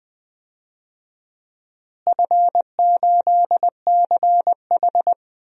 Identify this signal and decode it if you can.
Morse code, heard as a keyed tone, about 20 wpm, 700 Hz. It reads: F8CH
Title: DXCC-100 Call Signs - 4 Characters - Repeated 3x - 20wpm